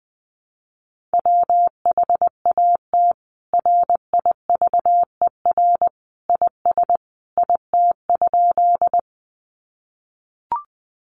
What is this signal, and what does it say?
Morse code, heard as a keyed tone, about 20 wpm, 700 Hz. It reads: WHAT RIVER IS IT?
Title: Sentences from Top 300 Words 20wpm